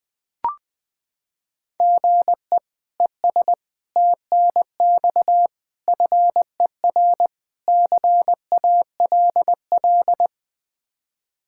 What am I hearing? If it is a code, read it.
Morse code, heard as a keyed tone, about 21 wpm, 700 Hz. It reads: GE ES TNX FER CALL
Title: K1USN SST QSO Practice 20wpm